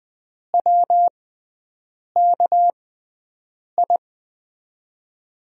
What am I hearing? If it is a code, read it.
Morse code, heard as a keyed tone, about 20 wpm, spaced as for 7 wpm, 700 Hz. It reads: WKI
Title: USA Call Signs - 6 Characters - 6x Character Spacing - 20wpm